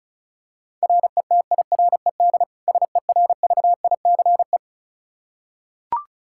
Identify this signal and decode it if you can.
Morse code, heard as a keyed tone, about 36 wpm, 700 Hz. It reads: RETIRED SERVICE
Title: Practice QSO - Exchange 3 - More Info - 35wpm